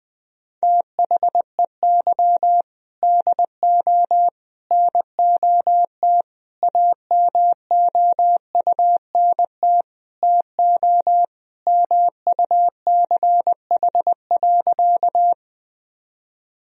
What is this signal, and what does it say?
Morse code, heard as a keyed tone, about 20 wpm, 700 Hz. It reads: THEY DO NOT AMOUNT TO MUCH.
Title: Sentences from Top 1000 Words - Review Part 4 - 20wpm